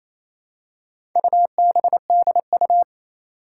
Morse code, heard as a keyed tone, about 27 wpm, 700 Hz. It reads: UBDU
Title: Four Challenging Letter-Number Combinations 28wpm